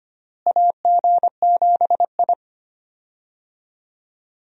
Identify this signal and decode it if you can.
Morse code, heard as a keyed tone, about 25 wpm, 700 Hz. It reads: AG7I
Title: USA Call Signs - 4 Characters - Repeated 3x - 25wpm